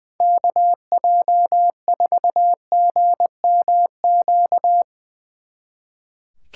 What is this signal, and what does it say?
Morse code, heard as a keyed tone, about 21 wpm, 700 Hz. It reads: KJ4GMQ
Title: Speed Racing - USA 2x3 Call Signs 20wpm